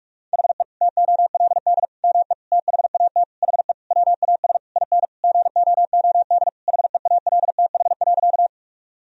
Morse code, heard as a keyed tone, about 41 wpm, 700 Hz. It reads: HE TOLD ME THAT HE WAS IN GOOD HEALTH.
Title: Sentences from Top 1000 Words - Review Part 3 - 45wpm